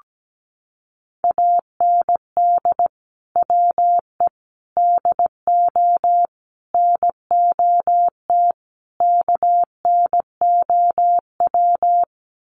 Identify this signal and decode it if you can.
Morse code, heard as a keyed tone, about 17 wpm, 700 Hz. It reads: AND WE DO NOT KNOW
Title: Sets of 5 Words Encore 17wpm